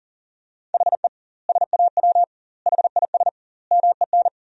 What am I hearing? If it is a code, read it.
Morse code, heard as a keyed tone, about 40 wpm, 700 Hz. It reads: HE SAW HIS MEN
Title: 2x Word-Spacing - Sentences from Top 300 Words - 40wpm